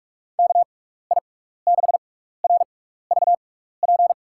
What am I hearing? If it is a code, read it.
Morse code, heard as a keyed tone, about 43 wpm, spaced as for 15 wpm, 700 Hz. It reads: KI6RVP
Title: USA Call Signs - 6 Characters - 6x Character Spacing - 45wpm